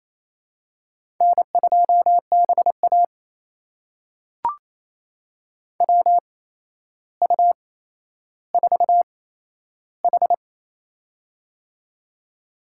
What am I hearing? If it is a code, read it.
Morse code, heard as a keyed tone, about 27 wpm, spaced as for 12 wpm, 700 Hz. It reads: N2BA WU4H
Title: USA Call Signs - 4 Characters - 8x Character Spacing - 28wpm